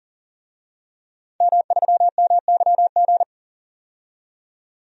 Morse code, heard as a keyed tone, about 40 wpm, 700 Hz. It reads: M3MYG
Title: DXCC-100 Call Signs - 5 Characters - 40wpm